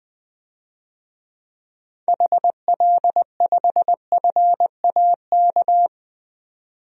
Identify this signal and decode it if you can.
Morse code, heard as a keyed tone, about 20 wpm, 700 Hz. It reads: HL5FAK
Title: DXCC-100 Call Signs - 6 Characters - 20wpm